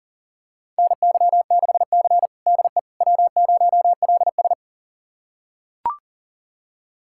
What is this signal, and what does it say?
Morse code, heard as a keyed tone, about 37 wpm, 700 Hz. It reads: NY6C DE W0LS
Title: USA Callsign DE Callsign - 4 Characters - 40wpm